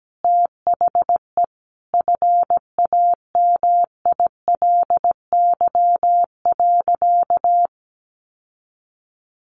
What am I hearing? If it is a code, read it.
Morse code, heard as a keyed tone, about 17 wpm, 700 Hz. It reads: THE FAMILY.
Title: Sentences from Top 1400 Words 17wpm